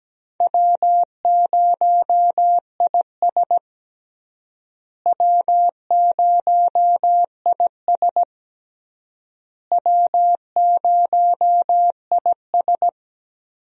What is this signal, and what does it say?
Morse code, heard as a keyed tone, about 17 wpm, 700 Hz. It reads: W0IS W0IS W0IS
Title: USA Call Signs - 4 Characters - Repeated 3x - 17wpm